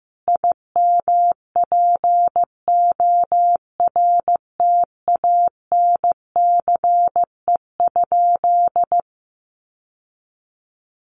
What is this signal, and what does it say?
Morse code, heard as a keyed tone, about 15 wpm, 700 Hz. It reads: IMPORTANCE?
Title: Sentences from Top 1200 Words 15wpm